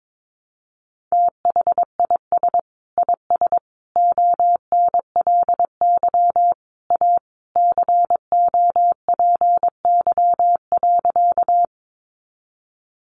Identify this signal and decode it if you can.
Morse code, heard as a keyed tone, about 22 wpm, 700 Hz. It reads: THIS IS ONLY A COPY.
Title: Sentences from Top 1400 Words 22wpm